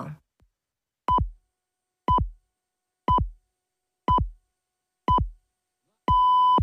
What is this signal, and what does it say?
Radio time signal (time pips).